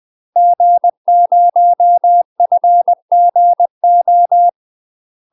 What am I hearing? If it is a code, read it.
Morse code, heard as a keyed tone, about 20 wpm, 700 Hz. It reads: G0FGO
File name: UK MASTER QSOs TRAINING / 20WPM / UK MASTER QSOs 0059